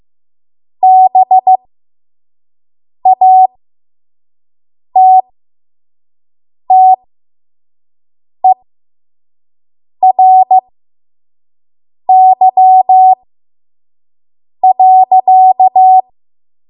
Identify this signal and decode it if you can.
Morse code, heard as a keyed tone, about 15 wpm, spaced as for 5 wpm, 750 Hz. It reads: BATTERY.
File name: Morse / 05WPM